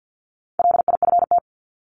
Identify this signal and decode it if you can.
Morse code, heard as a keyed tone, about 50 wpm, 700 Hz. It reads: LIFT
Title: Top 751-1000 Words 50wpm